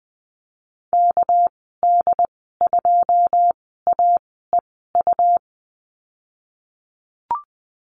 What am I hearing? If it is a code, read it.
Morse code, heard as a keyed tone, about 20 wpm, 700 Hz. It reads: K D 2 A E U
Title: 238 Lesson 38g - Letter q - Call Signs 20wpm